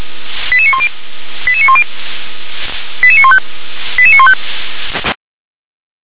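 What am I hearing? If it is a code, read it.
Five-tone selective call (ZVEI), heard as quick runs of five tones, tones of about 70 ms. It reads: 88910 88910 88915 88915
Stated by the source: ZVEI five-tone call received by SDR.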